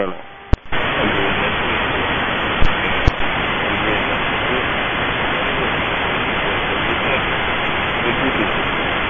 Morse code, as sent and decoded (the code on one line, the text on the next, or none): none